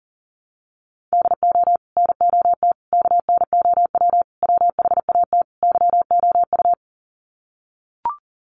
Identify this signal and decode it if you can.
Morse code, heard as a keyed tone, about 40 wpm, 700 Hz. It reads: DO NOT KNOW WHAT YOU
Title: Sets of 5 Words 40wpm